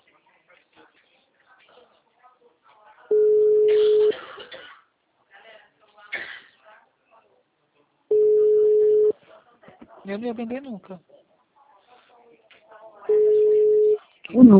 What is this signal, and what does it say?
Ringback tone (Europe, 425 Hz), with speech.
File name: out- -2041- - - .29971